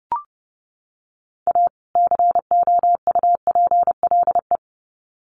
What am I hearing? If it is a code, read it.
Morse code, heard as a keyed tone, about 30 wpm, 700 Hz. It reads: A COUPLE
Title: Sets of 2 Words 30wpm